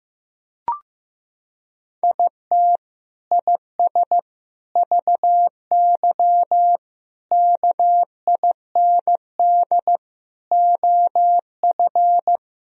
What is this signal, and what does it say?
Morse code, heard as a keyed tone, about 15 wpm, 700 Hz. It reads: IT IS VY KIND OF
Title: Sentences from Top 200 Words and Abbreviations 15wpm